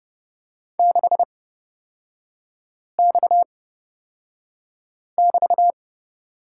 Morse code, heard as a keyed tone, about 30 wpm, spaced as for 5 wpm, 700 Hz. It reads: 6X<BT>
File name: Lesson 37e - Letter x - Three Characters 30wpm